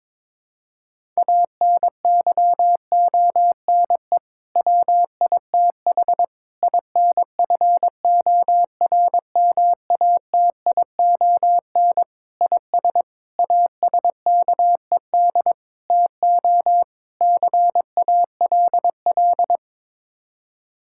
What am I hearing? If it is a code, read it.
Morse code, heard as a keyed tone, about 22 wpm, 700 Hz. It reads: ANYONE WITH INFORMATION IS ASKED TO CALL
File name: practice / 22wpm / Sets of 7 Words - 22wpm